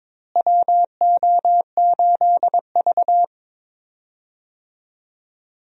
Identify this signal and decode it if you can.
Morse code, heard as a keyed tone, about 22 wpm, 700 Hz. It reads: WO8V
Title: USA Call Signs - 4 Characters - Repeated 2x - 22wpm